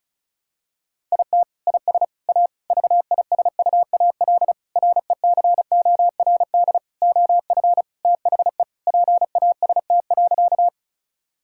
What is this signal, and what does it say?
Morse code, heard as a keyed tone, about 36 wpm, 700 Hz. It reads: IT IS A VISUAL RECORD OF THE PAST.
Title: Sentences from Top 1000 Words - Review Part 6 - 35wpm